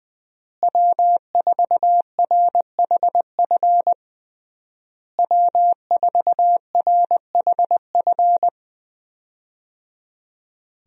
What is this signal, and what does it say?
Morse code, heard as a keyed tone, about 20 wpm, 700 Hz. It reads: W4RHF W4RHF
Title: USA Call Signs - 5 Characters - Repeated 3x - 20wpm